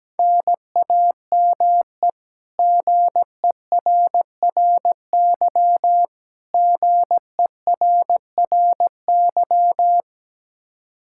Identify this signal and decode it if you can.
Morse code, heard as a keyed tone, about 17 wpm, 700 Hz. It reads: NAME GERRY GERRY
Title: Practice QSO - Exchange 1 - Beginning - 17wpm